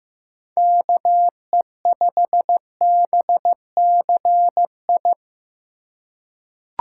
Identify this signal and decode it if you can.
Morse code, heard as a keyed tone, about 15 wpm, 700 Hz. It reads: KE5BCI
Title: USA Call Signs - 6 Characters - 4x Character Spacing - 15wpm